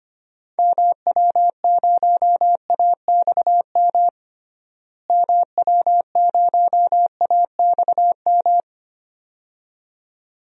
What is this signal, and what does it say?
Morse code, heard as a keyed tone, about 25 wpm, 700 Hz. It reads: MW0AXM MW0AXM
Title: DXCC-100 Call Signs - 6 Characters - Repeated 2x - 25wpm